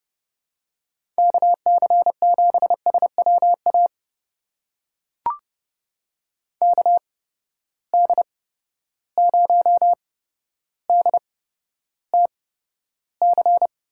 Morse code, heard as a keyed tone, about 30 wpm, spaced as for 13 wpm, 700 Hz. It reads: KC7SWA KD0DTC
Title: USA Call Signs - 6 Characters - 8x Character Spacing - 30wpm